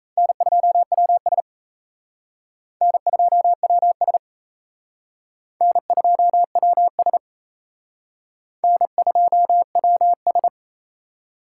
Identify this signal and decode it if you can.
Morse code, heard as a keyed tone, about 37 wpm, 700 Hz. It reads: N2WS N2WS N2WS N2WS